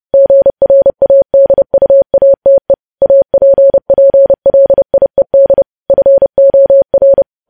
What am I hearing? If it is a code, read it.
Morse code, heard as a keyed tone, about 30 wpm, 550 Hz. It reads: GRADUATE APPLIED FOR
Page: news headlines in morse code at 30 wpm